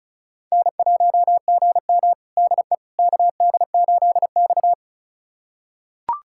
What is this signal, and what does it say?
Morse code, heard as a keyed tone, about 36 wpm, 700 Hz. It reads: N1GM DE KD8X
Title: USA Callsign DE Callsign - 4 Characters - 35wpm